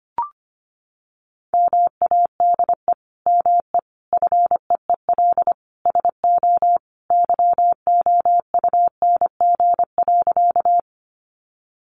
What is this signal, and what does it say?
Morse code, heard as a keyed tone, about 25 wpm, 700 Hz. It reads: MADE ME FEEL SO YOUNG.
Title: Sentences from Top 400 Words 25wpm